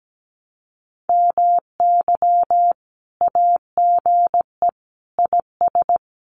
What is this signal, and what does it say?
Morse code, heard as a keyed tone, about 17 wpm, 700 Hz. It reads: MY AGE IS
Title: Common QSO Phrases Encore 17wpm